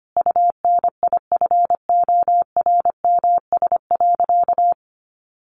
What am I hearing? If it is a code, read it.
Morse code, heard as a keyed tone, about 25 wpm, 700 Hz. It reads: UNIFORMS.
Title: Easy Phrases 25wpm